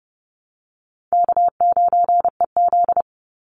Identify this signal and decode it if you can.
Morse code, heard as a keyed tone, about 30 wpm, 700 Hz. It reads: K9EZ
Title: NA SOTA Chaser Callsigns 30wpm